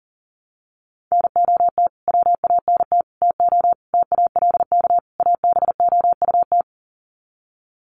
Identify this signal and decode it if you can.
Morse code, heard as a keyed tone, about 40 wpm, 700 Hz. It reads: NOT WANT TO TALK ABOUT
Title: Sets of 5 Words 40wpm